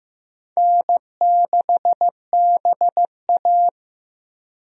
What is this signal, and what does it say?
Morse code, heard as a keyed tone, about 15 wpm, 700 Hz. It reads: N6BA
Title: USA Call Signs - 4 Characters - 15wpm